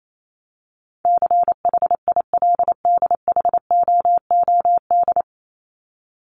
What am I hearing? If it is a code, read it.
Morse code, heard as a keyed tone, about 28 wpm, 700 Hz. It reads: CHILDHOOD
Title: Top 1751-2000 Words 28wpm